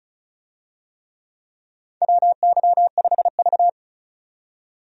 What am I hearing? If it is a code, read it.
Morse code, heard as a keyed tone, about 36 wpm, 700 Hz. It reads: WY5V
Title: USA Call Signs - 4 Characters - 35wpm